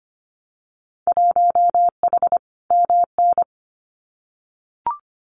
Morse code, heard as a keyed tone, about 25 wpm, 700 Hz. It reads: 1H MN
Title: Winter Field Day Exchanges Encore 25wpm